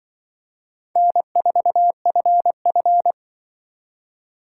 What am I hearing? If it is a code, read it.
Morse code, heard as a keyed tone, about 24 wpm, 700 Hz. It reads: N4FF
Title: Speed Racing - USA 1x2 Call Signs 20wpm